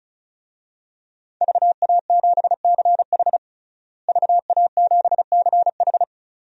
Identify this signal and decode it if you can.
Morse code, heard as a keyed tone, about 36 wpm, 700 Hz. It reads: VA7CH VA7CH
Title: DXCC-100 Call Signs - 5 Characters - Repeated 2x - 35wpm